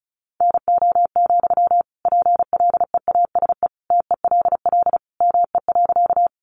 Morse code, heard as a keyed tone, about 34 wpm, 700 Hz. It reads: NO, PLEASE TELL ME.
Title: Sentences from Top 1500 Words 35wpm